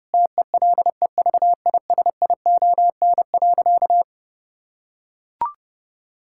Morse code, heard as a keyed tone, about 30 wpm, 700 Hz. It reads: TELEVISION.
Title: Sentences from Top 1000 Words - Review Part 5 - 30wpm